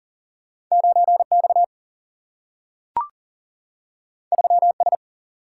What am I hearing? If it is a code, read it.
Morse code, heard as a keyed tone, about 39 wpm, 700 Hz. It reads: 9X 3S
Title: Single-Letter Single-Number 40wpm